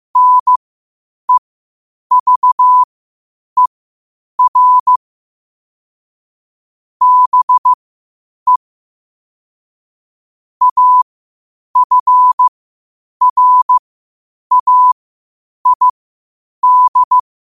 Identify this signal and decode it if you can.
Morse code, heard as a keyed tone, about 15 wpm, spaced as for 8 wpm, 1000 Hz. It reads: NEVER BE AFRAID
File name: qotdfiles / 8wpm